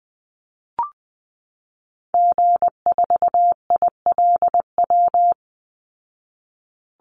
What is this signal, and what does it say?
Morse code, heard as a keyed tone, about 20 wpm, 700 Hz. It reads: G4ILW
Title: Call Signs Repeated 2x - 20wpm